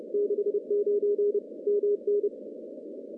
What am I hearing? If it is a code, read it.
Morse code, heard as a keyed tone, about 30 wpm, 410 Hz. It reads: 69 MN
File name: contests / CW-SS-k5tr- -1400